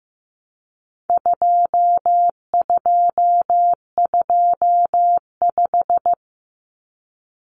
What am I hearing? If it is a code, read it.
Morse code, heard as a keyed tone, about 15 wpm, 700 Hz. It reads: 2225
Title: Common Cut Numbers - 4-Digit 15wpm